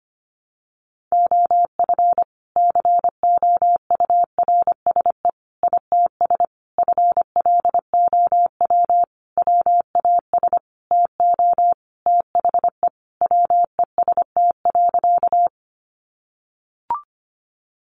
Morse code, heard as a keyed tone, about 25 wpm, 700 Hz. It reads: OF COURSE ITS FLOW WAS TO THE WEST.
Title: Sentences from Top 1200 Words 25wpm